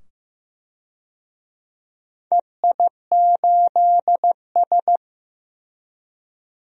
Morse code, heard as a keyed tone, about 15 wpm, 700 Hz. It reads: EI8S
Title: DXCC-100 Call Signs - 4 Characters - 15wpm